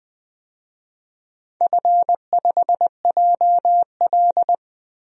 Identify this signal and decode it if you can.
Morse code, heard as a keyed tone, about 20 wpm, 700 Hz. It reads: F5JL
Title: DXCC-100 Call Signs - 4 Characters - 20wpm